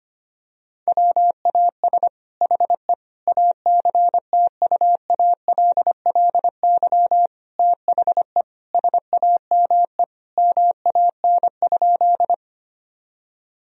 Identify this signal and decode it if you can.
Morse code, heard as a keyed tone, about 25 wpm, 700 Hz. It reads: WAS HE ACTUALLY THE SAME MAN?
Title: Sentences from Top 1000 Words - Review Part 4 - 25wpm